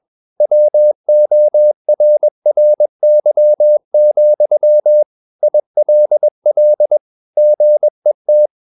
Morse code, heard as a keyed tone, about 21 wpm, 600 Hz. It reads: WORRY, ILL GET
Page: CW med Gnister